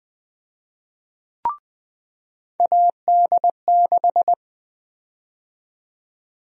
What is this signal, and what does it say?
Morse code, heard as a keyed tone, about 20 wpm, 700 Hz. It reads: AD6
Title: USA Call Sign Prefixes - 20wpm